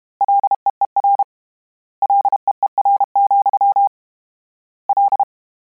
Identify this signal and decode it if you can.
Morse code, heard as a keyed tone, about 32 wpm, 800 Hz. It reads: LEER LEER, L
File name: Combinations / Combinations-mp3s / Combinations@30